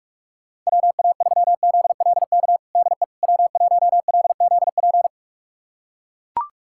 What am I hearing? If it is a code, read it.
Morse code, heard as a keyed tone, about 41 wpm, 700 Hz. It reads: WA3ZLK DE W1LZP
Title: USA Callsign DE Callsign 45wpm